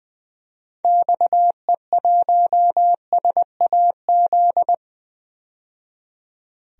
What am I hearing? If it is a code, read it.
Morse code, heard as a keyed tone, about 21 wpm, 700 Hz. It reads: XE1SAZ
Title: DXCC-100 Call Signs - 6 Characters - 20wpm